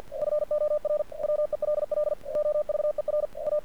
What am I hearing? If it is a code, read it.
Morse code, heard as a keyed tone, about 48 wpm, 600 Hz. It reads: FOR WELL OVER A
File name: archived / CRSnet